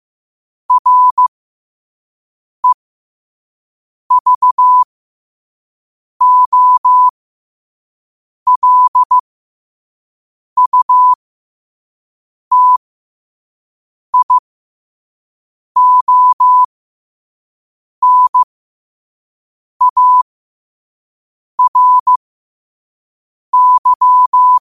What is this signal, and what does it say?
Morse code, heard as a keyed tone, about 15 wpm, spaced as for 5 wpm, 1000 Hz. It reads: REVOLUTIONARY
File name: qotdfiles / 5wpm